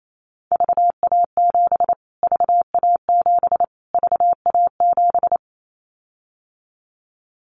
Morse code, heard as a keyed tone, about 28 wpm, 700 Hz. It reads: VA7 VA7 VA7
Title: Callsign Prefixes for Top 25 DXCC - Repeated 3x - 28wpm